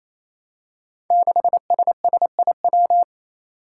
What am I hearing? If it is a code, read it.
Morse code, heard as a keyed tone, about 29 wpm, 700 Hz. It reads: 6SSIW
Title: Code Groups - 5 Letter:Numbers - 28wpm